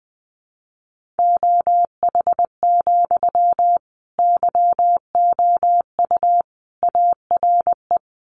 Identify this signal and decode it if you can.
Morse code, heard as a keyed tone, about 20 wpm, 700 Hz. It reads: OH, YOU ARE
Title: Sentences from Top 500 Words 20wpm